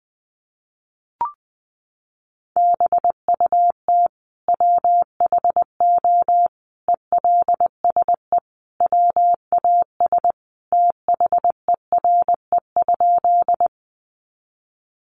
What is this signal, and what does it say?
Morse code, heard as a keyed tone, about 20 wpm, 700 Hz. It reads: BUT WHO ELSE WAS THERE?
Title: Sentences from Top 800 Words 20wpm